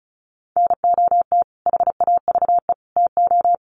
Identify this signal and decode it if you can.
Morse code, heard as a keyed tone, about 34 wpm, 700 Hz. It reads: NOT HAVE TO
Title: Sets of 3 Words 35wpm